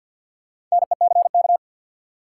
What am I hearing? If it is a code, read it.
Morse code, heard as a keyed tone, about 53 wpm, 700 Hz. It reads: NECK
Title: Sets of 5 Words Encore - New Word Drill - 50wpm